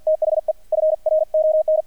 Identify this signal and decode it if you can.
Morse code, heard as a keyed tone, about 48 wpm, 620 Hz, with radio noise. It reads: THE FRON
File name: archived / CRSnet